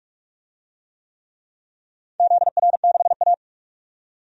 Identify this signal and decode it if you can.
Morse code, heard as a keyed tone, about 44 wpm, 700 Hz. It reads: ZR6A
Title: DXCC-100 Call Signs - 4 Characters - 45wpm